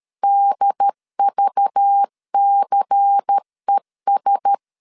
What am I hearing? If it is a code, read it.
Morse code, heard as a keyed tone, about 13 wpm, 800 Hz. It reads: DVCES
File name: CW / DET / 1 9 1500C S B 12wpm 3819